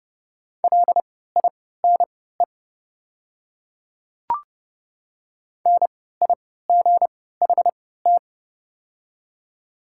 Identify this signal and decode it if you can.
Morse code, heard as a keyed tone, about 30 wpm, spaced as for 17 wpm, 700 Hz. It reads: LINE NIGHT
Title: 3x Character Spacing - Top 500 Words - 30wpm